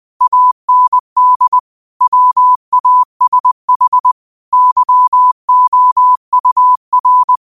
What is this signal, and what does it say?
Morse code, heard as a keyed tone, about 20 wpm, 1000 Hz. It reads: AND WASH YOUR
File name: 20wpm